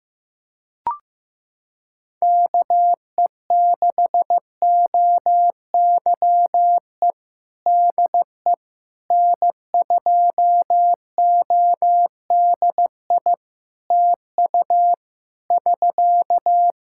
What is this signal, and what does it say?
Morse code, heard as a keyed tone, about 15 wpm, 700 Hz. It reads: KE6OYE DE N2ODI TU <SK>
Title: K1USN SST QSO Practice 15wpm